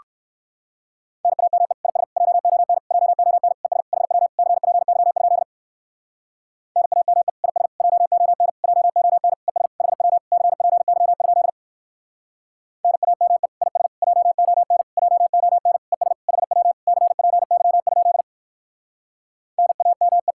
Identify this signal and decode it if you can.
Morse code, heard as a keyed tone, about 58 wpm, 700 Hz. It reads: NAME IS JON JON ES HW CPY? NAME IS JON JON ES HW CPY? NAME IS JON JON ES HW CPY? NAME